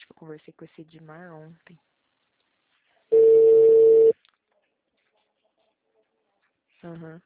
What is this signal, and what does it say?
Ringback tone (China, 450 Hz), with speech.